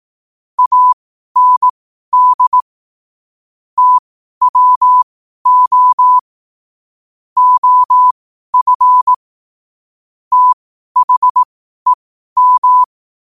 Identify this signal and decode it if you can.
Morse code, heard as a keyed tone, about 18 wpm, spaced as for 12 wpm, 1000 Hz. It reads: AND TWO OF THEM
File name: qotdfiles / 12wpm